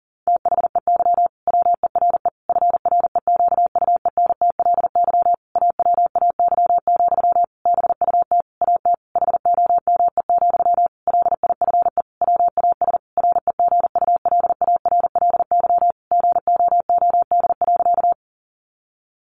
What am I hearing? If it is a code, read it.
Morse code, heard as a keyed tone, about 40 wpm, 700 Hz. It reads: THEY WERE FREQUENTLY AWAY, BUT AT HOME, LIFE WAS REGULARLY GOOD.
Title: Sentences from Top 1400 Words 40wpm